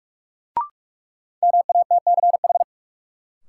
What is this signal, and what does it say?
Morse code, heard as a keyed tone, about 46 wpm, 700 Hz. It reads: MATCH
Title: Sentences from Top 1600 Words - New Word Drill Alt - 45wpm